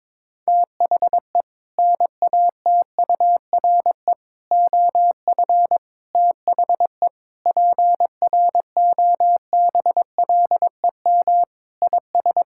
Morse code, heard as a keyed tone, about 22 wpm, 700 Hz. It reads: THE NATURE OF THE PROBLEM IS
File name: practice / 22wpm / Sentences with 4-Word Phrases - 22wpm